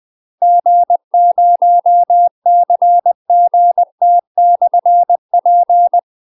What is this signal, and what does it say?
Morse code, heard as a keyed tone, about 20 wpm, 700 Hz. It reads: G0CGT/P